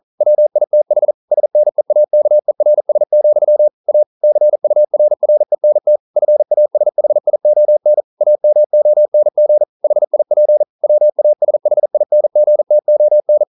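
Morse code, heard as a keyed tone, about 38 wpm, 600 Hz. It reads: WITH SNEAKERS, A CURRENT FASHION AMONG HIP WASHINGTON
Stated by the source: Morse code practice at 41 wpm, from CW med Gnister.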